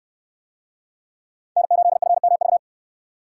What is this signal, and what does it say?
Morse code, heard as a keyed tone, about 61 wpm, 700 Hz. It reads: N8FMV